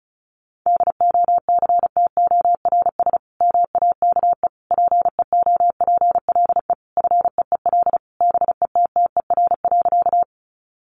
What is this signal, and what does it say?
Morse code, heard as a keyed tone, about 34 wpm, 700 Hz. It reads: DOCTORS MAKE PEOPLE FEEL BETTER.
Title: Easy Phrases 35wpm